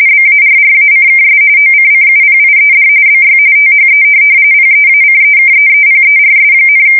RTTY, about 45.45 baud. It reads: BARRACKS OCCUPIED BY THE PEACEKEEPERS,
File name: fsk170 / USS-New-Jersey